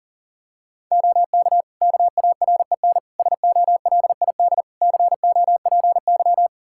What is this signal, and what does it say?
Morse code, heard as a keyed tone, about 37 wpm, 700 Hz. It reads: OK KAREN SOLID COPY
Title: Practice QSO - Exchange 3 - More Info - 40wpm